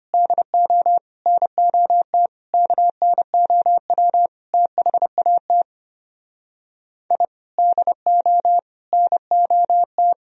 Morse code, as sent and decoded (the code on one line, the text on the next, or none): -.. --- / -. --- - / -.- -. --- .-- / - .... .- - / .. / -.. --- / -. --- -
DO NOT KNOW THAT I DO NOT